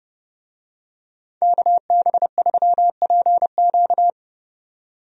Morse code, heard as a keyed tone, about 30 wpm, 700 Hz. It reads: KB3PQ
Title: USA Call Signs - 5 Characters - 30wpm